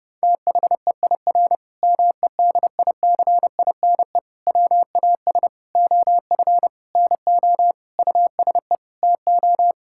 Morse code, heard as a keyed tone, about 32 wpm, 700 Hz. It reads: THEIR MEDICINE WAS OF NO USE TO
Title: Sentences from Top 1000 Words - Review Part 6 - 30wpm